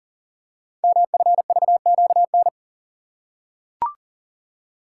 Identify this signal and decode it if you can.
Morse code, heard as a keyed tone, about 38 wpm, 690 Hz. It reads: MFVQN
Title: Code Groups - 5 Letters - 40wpm